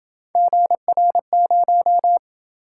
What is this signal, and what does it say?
Morse code, heard as a keyed tone, about 27 wpm, 700 Hz. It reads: GR0